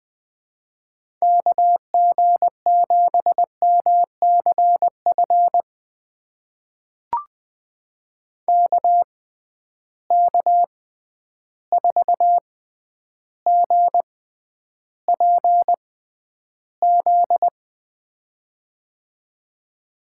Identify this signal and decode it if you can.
Morse code, heard as a keyed tone, about 20 wpm, spaced as for 10 wpm, 700 Hz. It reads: KG7MCF KK4GPZ
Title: USA Call Signs - 6 Characters - 6x Character Spacing - 20wpm